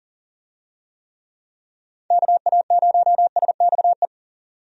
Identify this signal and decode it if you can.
Morse code, heard as a keyed tone, about 40 wpm, 700 Hz. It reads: KA0SXE
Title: USA Call Signs - 6 Characters - 40wpm